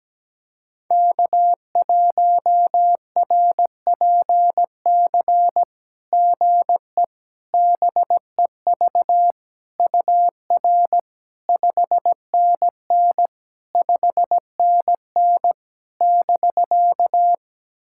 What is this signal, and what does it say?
Morse code, heard as a keyed tone, about 17 wpm, 700 Hz. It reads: K1RPC GE BEV UR 5NN 5NN <BK>
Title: SOTA QSO Practice 17wpm